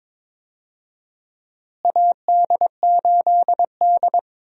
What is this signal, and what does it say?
Morse code, heard as a keyed tone, about 22 wpm, 700 Hz. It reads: AD8D
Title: USA Call Signs - 4 Characters - Repeated 2x - 22wpm